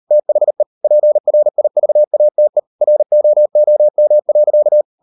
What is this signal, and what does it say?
Morse code, heard as a keyed tone, about 38 wpm, 600 Hz. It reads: THE PRIVATE ROOM.